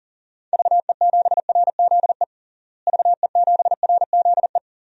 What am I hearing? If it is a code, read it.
Morse code, heard as a keyed tone, about 37 wpm, 700 Hz. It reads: VE7RZE VE7RZE
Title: DXCC-100 Call Signs - 6 Characters - Repeated 3x - 40wpm